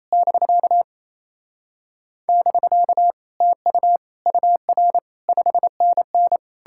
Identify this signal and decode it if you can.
Morse code, heard as a keyed tone, about 37 wpm, 700 Hz. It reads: <BK> <BK> TU UR 5NN